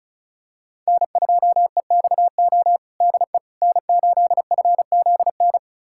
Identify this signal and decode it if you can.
Morse code, heard as a keyed tone, about 36 wpm, 700 Hz. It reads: N2EXO DE N8FZN
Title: USA Callsign DE Callsign - 5 Characters - 35wpm